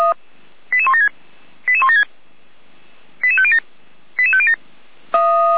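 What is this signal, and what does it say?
Five-tone selective call (ZVEI), heard as quick runs of five tones, tones of about 70 ms. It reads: 88176 88176 88507 88507
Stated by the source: ZVEI five-tone call received on SDR, recorded by Feuerwehr Orlamünde.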